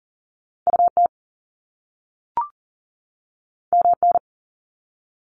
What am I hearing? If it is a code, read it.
Morse code, heard as a keyed tone, about 38 wpm, 700 Hz. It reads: UT MN
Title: US States 40wpm